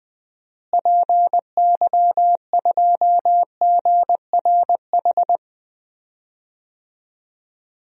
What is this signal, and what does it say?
Morse code, heard as a keyed tone, about 20 wpm, 700 Hz. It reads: PY2GRH
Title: DXCC-100 Call Signs - 6 Characters - Repeated 2x - 20wpm